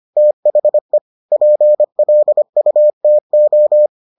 Morse code, heard as a keyed tone, about 25 wpm, 600 Hz. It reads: THE PLUTO